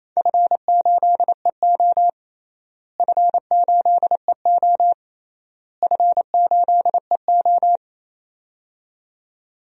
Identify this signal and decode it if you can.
Morse code, heard as a keyed tone, about 28 wpm, 700 Hz. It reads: F8EO F8EO F8EO
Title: DXCC-100 Call Signs - 4 Characters - Repeated 3x - 28wpm